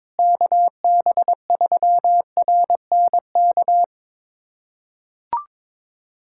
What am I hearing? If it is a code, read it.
Morse code, heard as a keyed tone, about 22 wpm, 700 Hz. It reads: KB3RNK
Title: USA Callsign DE Callsign 22wpm